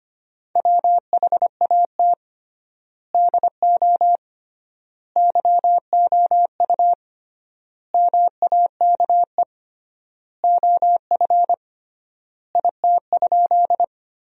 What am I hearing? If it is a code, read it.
Morse code, heard as a keyed tone, about 25 wpm, 700 Hz. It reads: WHAT DO YOU MAKE OF IT?
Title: 3x Word-Spacing - Sentences from Top 100 Words - 25wpm